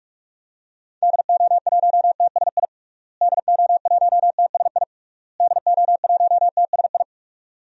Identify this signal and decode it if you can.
Morse code, heard as a keyed tone, about 41 wpm, 700 Hz. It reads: DO1TSI DO1TSI DO1TSI
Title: DXCC-100 Call Signs - 6 Characters - Repeated 3x - 45wpm